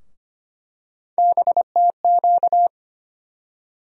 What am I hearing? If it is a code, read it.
Morse code, heard as a keyed tone, about 25 wpm, 700 Hz. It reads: BTQ
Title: USA Call Sign Postfixes - 3 Letters 25wpm